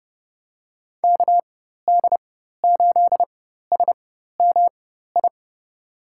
Morse code, heard as a keyed tone, about 30 wpm, spaced as for 14 wpm, 700 Hz. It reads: KD8SMI
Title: USA Call Signs - 6 Characters - 4x Character Spacing - 30wpm